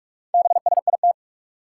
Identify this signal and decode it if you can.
Morse code, heard as a keyed tone, about 47 wpm, 700 Hz. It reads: BSIT
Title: Four Challenging Letter-Number Combinations 45wpm